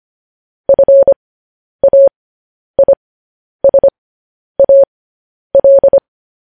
Morse code, heard as a keyed tone, about 25 wpm, spaced as for 10 wpm, 550 Hz. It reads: FAISAL
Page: news headlines in morse code at 10 wpm